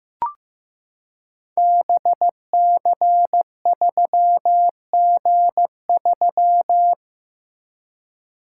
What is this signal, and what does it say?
Morse code, heard as a keyed tone, about 15 wpm, 700 Hz. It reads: BC3G3
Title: Code Groups - 5 Letter:Numbers - 15wpm